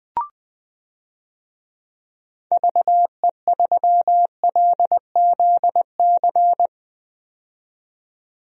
Morse code, heard as a keyed tone, about 20 wpm, 700 Hz. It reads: VE3LZC
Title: DXCC-100 Call Signs - 6 Characters - 20wpm